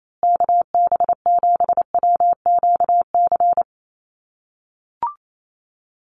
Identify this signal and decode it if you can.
Morse code, heard as a keyed tone, about 28 wpm, 700 Hz. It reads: KB7WQC
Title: USA Call Signs Repeated 2x 28wpm